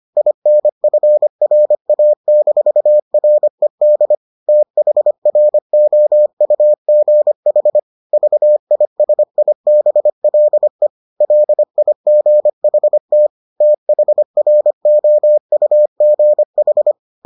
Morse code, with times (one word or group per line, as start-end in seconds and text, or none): 0.2-4.2 INFRA-RED
4.5-7.8 THROUGH
8.1-10.9 VISIBLE
11.2-13.3 LIGHT
13.6-16.9 THROUGH